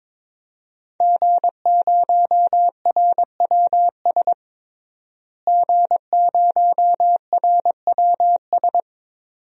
Morse code, heard as a keyed tone, about 22 wpm, 700 Hz. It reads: G0RWS G0RWS
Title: DXCC-100 Call Signs - 5 Characters - Repeated 2x - 22wpm